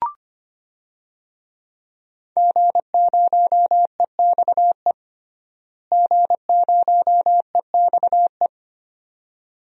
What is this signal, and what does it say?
Morse code, heard as a keyed tone, about 25 wpm, 700 Hz. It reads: G0EXE G0EXE
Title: DXCC-100 Call Signs - 5 Characters - Repeated 2x - 25wpm